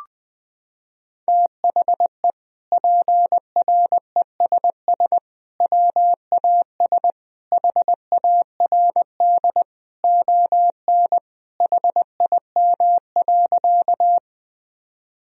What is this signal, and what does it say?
Morse code, heard as a keyed tone, about 20 wpm, 700 Hz. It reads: THE PRESS WAS HARD ON HIM.
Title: Sentences from Top 1000 Words - Review Part 1 - 20wpm